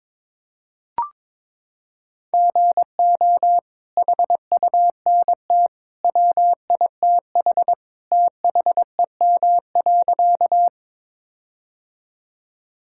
Morse code, heard as a keyed tone, about 21 wpm, 700 Hz. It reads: GO HUNT WITH THEM.
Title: Sentences from Top 1600 Words 22wpm